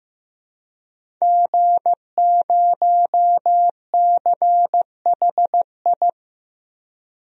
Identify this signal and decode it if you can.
Morse code, heard as a keyed tone, about 15 wpm, 700 Hz. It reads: G0CHI
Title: DXCC-100 Call Signs - 5 Characters - Repeated 2x - 15wpm